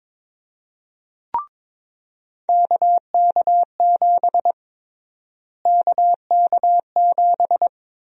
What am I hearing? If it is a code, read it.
Morse code, heard as a keyed tone, about 22 wpm, 700 Hz. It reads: KK7 KK7
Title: USA Call Sign Prefixes - Repeated 2x - 22wpm